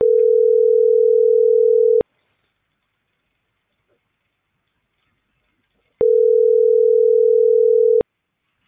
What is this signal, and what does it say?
Ringback tone (North America), 440 + 480 Hz.